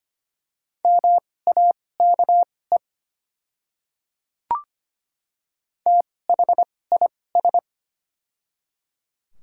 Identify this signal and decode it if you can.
Morse code, heard as a keyed tone, about 25 wpm, spaced as for 18 wpm, 700 Hz. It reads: MAKE THIS
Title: 2x Character Spacing - Top 500 Words - 25wpm